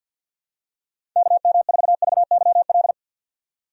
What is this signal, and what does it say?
Morse code, heard as a keyed tone, about 51 wpm, 700 Hz. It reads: KM4VYL
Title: USA Call Signs - 6 Characters - 50wpm